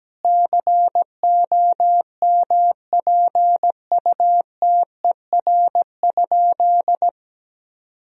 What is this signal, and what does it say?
Morse code, heard as a keyed tone, about 17 wpm, 700 Hz. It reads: COMPUTER?
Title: Sentences from Top 1000 Words - Review Part 3 - 17wpm